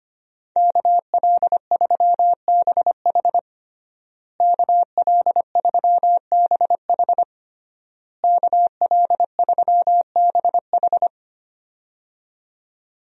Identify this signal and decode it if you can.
Morse code, heard as a keyed tone, about 25 wpm, 700 Hz. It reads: KL3BH KL3BH KL3BH
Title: USA Call Signs - 5 Characters - Repeated 3x - 25wpm